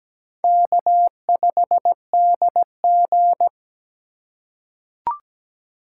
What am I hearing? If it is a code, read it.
Morse code, heard as a keyed tone, about 17 wpm, 700 Hz. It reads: K5DG
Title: USA Call Signs - 6x Character Spacing - 17wpm